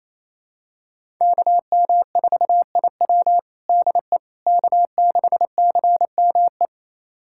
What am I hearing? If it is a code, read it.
Morse code, heard as a keyed tone, about 28 wpm, 700 Hz. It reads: KM4IW DE K6CME
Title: USA Callsign DE Callsign - 5 Characters - 28wpm